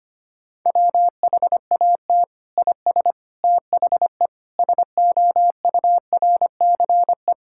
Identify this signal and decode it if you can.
Morse code, heard as a keyed tone, about 25 wpm, 700 Hz. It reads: WHAT IS THE SOURCE
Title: Sentences from Top 1000 Words - Review Part 5 - 25wpm